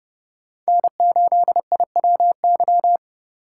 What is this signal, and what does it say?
Morse code, heard as a keyed tone, about 30 wpm, 700 Hz. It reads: N8IWY
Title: K1USN SST QSO Practice - 30wpm